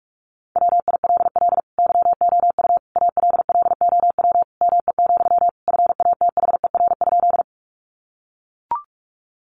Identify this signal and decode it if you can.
Morse code, heard as a keyed tone, about 44 wpm, 700 Hz. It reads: WILL YOU ALLOW ME, FATHER?
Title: Sentences from Top 700 Words 45wpm